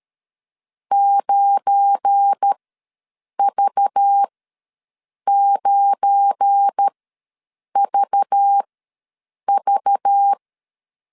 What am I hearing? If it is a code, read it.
Morse code, heard as a keyed tone, about 13 wpm, spaced as for 7 wpm, 800 Hz. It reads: 9V9VV
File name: LEZIONI / 8 YJ45 / 8 Y ws 12wpm 3409